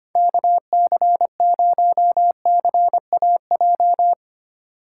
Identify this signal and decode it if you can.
Morse code, heard as a keyed tone, about 31 wpm, 700 Hz. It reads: KC0CAJ
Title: Speed Racing - USA 2x3 Call Signs 25wpm